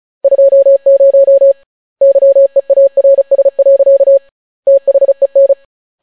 Morse code, heard as a keyed tone, about 36 wpm, 550 Hz. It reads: 20 YEARS. THEN